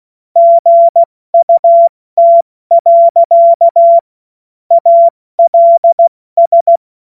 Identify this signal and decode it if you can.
Morse code, heard as a keyed tone, about 16 wpm, 680 Hz. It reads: GUT. ALS
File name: Morsekurs / Witze / Witz 21. Dezember 2021, 70 F80